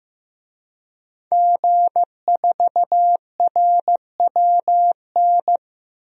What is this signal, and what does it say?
Morse code, heard as a keyed tone, about 15 wpm, 700 Hz. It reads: G4RWN
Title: DXCC-100 Call Signs - 5 Characters - 15wpm